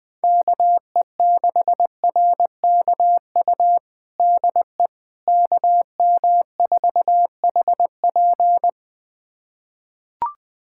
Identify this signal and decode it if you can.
Morse code, heard as a keyed tone, about 20 wpm, 700 Hz. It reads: KE6RKU DE KM4HP
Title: USA Callsign DE Callsign 20wpm